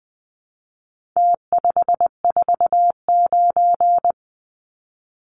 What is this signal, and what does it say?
Morse code, heard as a keyed tone, about 20 wpm, 700 Hz. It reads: T549
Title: Numbers - Contest Style - 4-Digit 20wpm